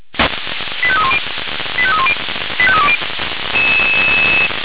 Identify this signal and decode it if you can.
Five-tone selective call (ZVEI), heard as quick runs of five tones, tones of about 70 ms. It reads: 85310 85310 85320